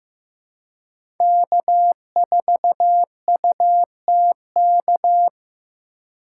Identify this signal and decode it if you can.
Morse code, heard as a keyed tone, about 15 wpm, 700 Hz. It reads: K4UTK
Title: USA Call Signs - 5 Characters - 4x Character Spacing - 15wpm